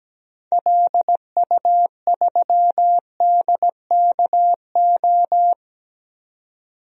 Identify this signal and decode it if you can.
Morse code, heard as a keyed tone, about 17 wpm, 700 Hz. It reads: LU3DKO
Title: DXCC-100 Call Signs - 6 Characters - 17wpm